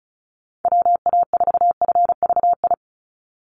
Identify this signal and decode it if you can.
Morse code, heard as a keyed tone, about 34 wpm, 700 Hz. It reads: WA4FVI
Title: USA Call Signs 35wpm